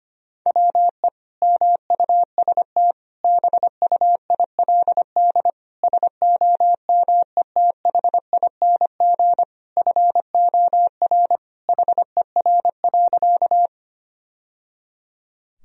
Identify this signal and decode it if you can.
Morse code, heard as a keyed tone, about 25 wpm, 700 Hz. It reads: WE MUST BUILD SOMETHING FOR HER.
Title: Sentences from Top 1000 Words - Review Part 1 - 25wpm